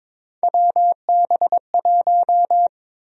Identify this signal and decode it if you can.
Morse code, heard as a keyed tone, about 22 wpm, 700 Hz. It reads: WB1
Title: USA Call Sign Prefixes - Repeated 2x - 22wpm